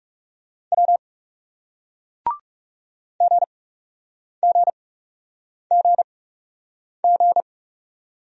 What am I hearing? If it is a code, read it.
Morse code, heard as a keyed tone, about 39 wpm, spaced as for 8 wpm, 700 Hz. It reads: W GGGG